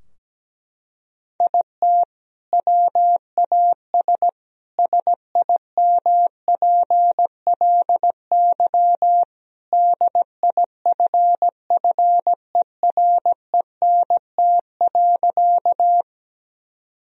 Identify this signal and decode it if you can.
Morse code, heard as a keyed tone, about 17 wpm, 700 Hz. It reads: IT WAS SIMPLY DIFFERENT.
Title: Sentences from Top 1000 Words - Review Part 4 - 17wpm